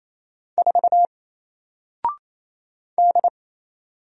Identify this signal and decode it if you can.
Morse code, heard as a keyed tone, about 28 wpm, 700 Hz. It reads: 4 D
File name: Single-Letter-Number-28wpm